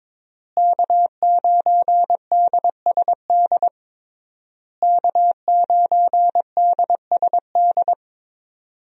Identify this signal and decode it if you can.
Morse code, heard as a keyed tone, about 22 wpm, 700 Hz. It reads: K9DSD K9DSD
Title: USA Call Signs - 5 Characters - Repeated 3x - 22wpm